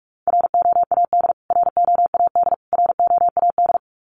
Morse code, heard as a keyed tone, about 44 wpm, 700 Hz. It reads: ROAD ROAD ROAD
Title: Top 500 Words Repeated 45wpm